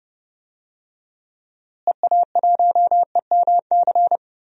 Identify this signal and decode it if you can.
Morse code, heard as a keyed tone, about 32 wpm, 700 Hz. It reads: EA1EMC
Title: DXCC-100 Call Signs - 6 Characters - Repeated 3x - 30wpm